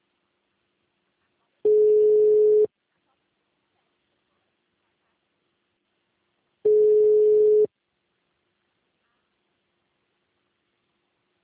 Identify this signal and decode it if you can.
Ringback tone (Europe, 425 Hz).